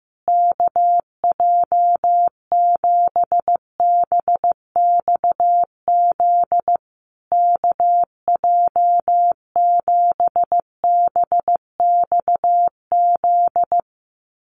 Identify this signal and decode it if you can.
Morse code, heard as a keyed tone, about 15 wpm, 700 Hz. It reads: KJ7BXZ KJ7BXZ
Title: USA Call Signs Repeated 3x 15wpm